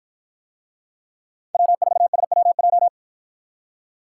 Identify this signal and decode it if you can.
Morse code, heard as a keyed tone, about 50 wpm, 700 Hz. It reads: W4SWJ